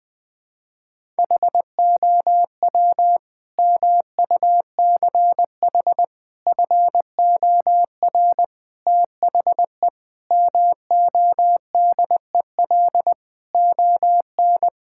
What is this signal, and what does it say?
Morse code, heard as a keyed tone, about 20 wpm, 700 Hz. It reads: HOW MUCH FOR THE MODEL ON
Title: Sentences from Top 1000 Words - Review Part 5 - 20wpm